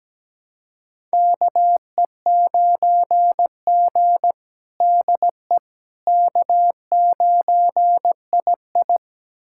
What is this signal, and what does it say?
Morse code, heard as a keyed tone, about 17 wpm, 700 Hz. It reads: KE9G DE K9II
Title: USA Callsign DE Callsign - 4 Characters - 17wpm